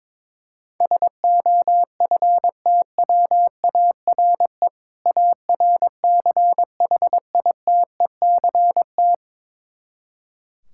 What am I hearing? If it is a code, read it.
Morse code, heard as a keyed tone, about 22 wpm, 700 Hz. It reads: SOFTWARE ARCHITECT
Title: Practice QSO - Exchange 3 - More Info - 22wpm